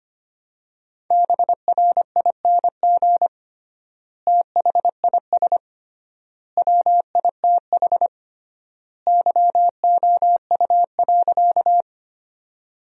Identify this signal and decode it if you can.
Morse code, heard as a keyed tone, about 25 wpm, 700 Hz. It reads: BRING THIS WITH YOU.
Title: 3x Word-Spacing - Sentences from Top 500 Words - 25wpm